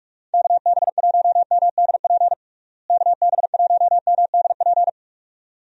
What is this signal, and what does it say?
Morse code, heard as a keyed tone, about 46 wpm, 700 Hz. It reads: KB1MDP KB1MDP
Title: USA Call Signs - 6 Characters - Repeated 2x - 45wpm